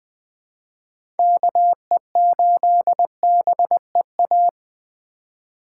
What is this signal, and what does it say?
Morse code, heard as a keyed tone, about 20 wpm, 700 Hz. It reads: KE8BEA
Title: USA Call Signs - 4x Character Spacing - 20wpm